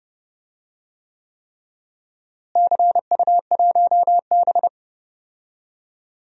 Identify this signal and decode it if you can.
Morse code, heard as a keyed tone, about 30 wpm, 700 Hz. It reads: CU1B
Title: DXCC-100 Call Signs - 4 Characters - Repeated 3x - 30wpm